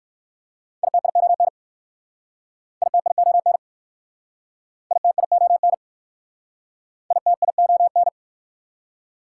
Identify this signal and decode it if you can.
Morse code, heard as a keyed tone, about 60 wpm, 700 Hz. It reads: ITION ITION ITION ITION